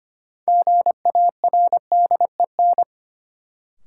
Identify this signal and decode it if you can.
Morse code, heard as a keyed tone, about 25 wpm, 700 Hz. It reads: GARDEN
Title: Sentences from Top 700 Words - New Word Drill - 25wpm